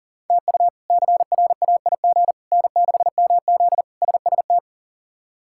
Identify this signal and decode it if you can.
Morse code, heard as a keyed tone, about 37 wpm, 700 Hz. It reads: TU CRAIG N6MZ SST
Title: K1USN SST QSO Practice - 40wpm